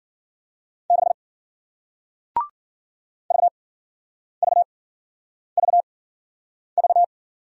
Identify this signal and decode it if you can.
Morse code, heard as a keyed tone, about 56 wpm, spaced as for 9 wpm, 700 Hz. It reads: 6 VVVV